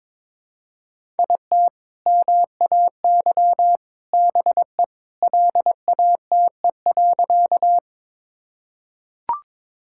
Morse code, heard as a keyed tone, about 22 wpm, 700 Hz. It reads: IT MAY BE LATE.
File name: practice / 22wpm / Sentences with 2-Word Phrases - 22wpm